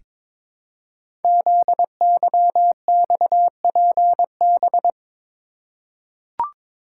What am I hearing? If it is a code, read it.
Morse code, heard as a keyed tone, about 22 wpm, 700 Hz. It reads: ZYXPB
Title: Code Groups - 5 Letter:Numbers - 22wpm